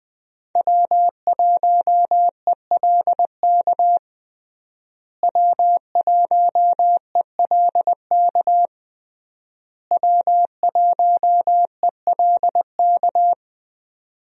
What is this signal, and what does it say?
Morse code, heard as a keyed tone, about 20 wpm, 700 Hz. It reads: W1ELK W1ELK W1ELK
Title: USA Call Signs - 5 Characters - Repeated 3x - 20wpm